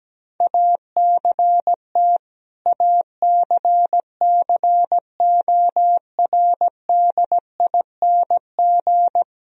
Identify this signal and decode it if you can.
Morse code, heard as a keyed tone, about 17 wpm, 700 Hz. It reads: ACT ACCORDING
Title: Sentences from Top 1000 Words - Review Part 4 - 17wpm